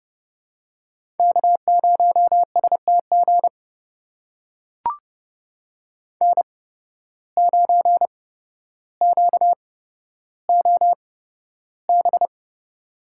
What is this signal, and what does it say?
Morse code, heard as a keyed tone, about 30 wpm, spaced as for 13 wpm, 700 Hz. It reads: K0STG N9QOB
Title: USA Call Signs - 5 Characters - 8x Character Spacing - 30wpm